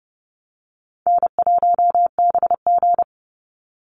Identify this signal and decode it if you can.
Morse code, heard as a keyed tone, about 30 wpm, 700 Hz. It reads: N1BG
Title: Easy Call Signs - 30wpm